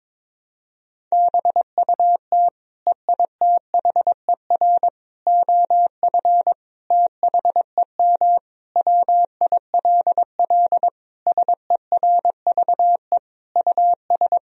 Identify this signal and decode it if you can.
Morse code, heard as a keyed tone, about 22 wpm, 700 Hz. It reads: BUT EITHER OF THEM WILL SERVE US
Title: Sentences from Top 1000 Words - Review Part 4 - 22wpm